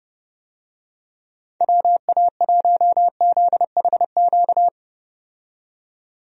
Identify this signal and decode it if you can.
Morse code, heard as a keyed tone, about 30 wpm, 700 Hz. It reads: WA1ZHQ
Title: USA Call Signs - 6 Characters - 30wpm